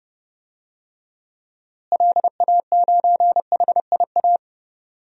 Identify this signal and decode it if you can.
Morse code, heard as a keyed tone, about 30 wpm, 700 Hz. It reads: LA9HIA
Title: DXCC-100 Call Signs - 6 Characters - Repeated 3x - 30wpm